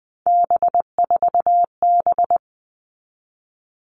Morse code, heard as a keyed tone, about 20 wpm, 700 Hz. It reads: B4B
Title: All Cut Numbers - 3-Digit 20wpm